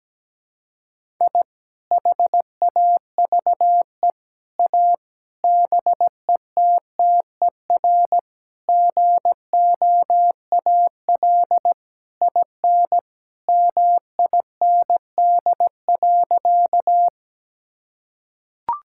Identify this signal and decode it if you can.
Morse code, heard as a keyed tone, about 17 wpm, 700 Hz. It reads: I HAVE A BETTER GOAL IN MIND.
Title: Sentences from Top 1000 Words - Review Part 5 - 17wpm